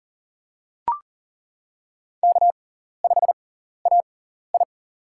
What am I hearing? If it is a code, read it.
Morse code, heard as a keyed tone, about 42 wpm, spaced as for 14 wpm, 700 Hz. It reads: K5AI
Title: USA Call Signs - 4 Characters - 6x Character Spacing - 40wpm